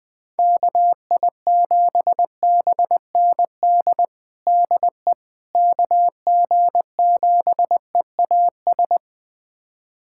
Morse code, heard as a keyed tone, about 20 wpm, 700 Hz. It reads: KI7BND DE KG7EAS
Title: USA Callsign DE Callsign - 6 Characters - 20wpm